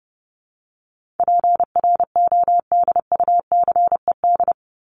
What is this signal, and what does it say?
Morse code, heard as a keyed tone, about 30 wpm, 700 Hz. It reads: PRODUCED
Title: ED Suffix - Top 100 Words with Base Words 30wpm